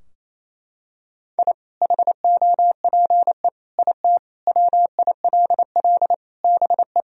Morse code, heard as a keyed tone, about 28 wpm, 700 Hz. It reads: I HOPE IT WILL BE
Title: Sentences from Top 1000 Words - Review Part 2 - 28wpm